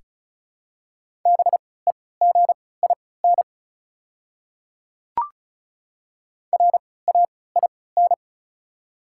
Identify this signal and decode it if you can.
Morse code, heard as a keyed tone, about 36 wpm, spaced as for 20 wpm, 700 Hz. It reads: BEGIN RAIN
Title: 3x Character Spacing - Top 300 Words - 35wpm